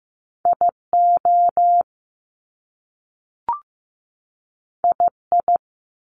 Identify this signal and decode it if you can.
Morse code, heard as a keyed tone, about 15 wpm, 700 Hz. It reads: IO II
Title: Top Two Letter Combinations in Words 15wpm